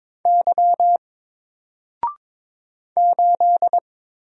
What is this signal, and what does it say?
Morse code, heard as a keyed tone, about 22 wpm, 700 Hz. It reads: Y 8
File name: Single-Letter-Number-22wpm